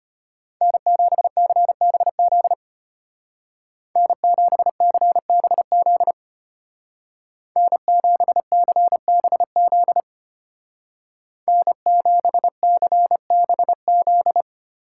Morse code, heard as a keyed tone, about 31 wpm, 700 Hz. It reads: N7CBZ N7CBZ N7CBZ N7CBZ
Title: Speed Racing - USA Call Signs 25wpm